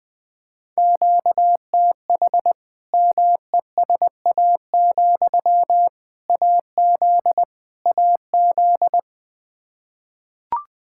Morse code, heard as a keyed tone, about 20 wpm, 700 Hz. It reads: QTH MESA, AZ AZ
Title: Practice QSO - Exchange 1 - Beginning - 20wpm